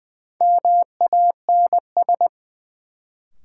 Morse code, heard as a keyed tone, about 20 wpm, 700 Hz. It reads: MANS
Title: Sentences from Top 1000 Words - Review Part 6 - New Word Drill - 20wpm